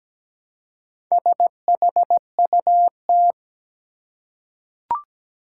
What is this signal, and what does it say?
Morse code, heard as a keyed tone, about 17 wpm, 700 Hz. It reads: SHUT
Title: Four Letter Words - 17wpm